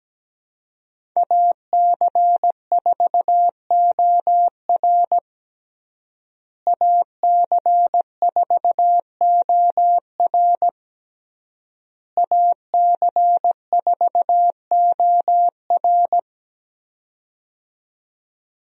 Morse code, heard as a keyed tone, about 17 wpm, 700 Hz. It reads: AC4OR AC4OR AC4OR
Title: USA Call Signs - 5 Characters - Repeated 3x - 17wpm